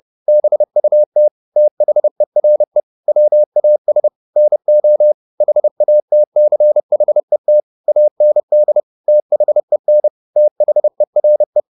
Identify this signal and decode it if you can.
Morse code, heard as a keyed tone, about 30 wpm, 600 Hz. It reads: BUT THERE WAS NO HATCHET AND THEN THERE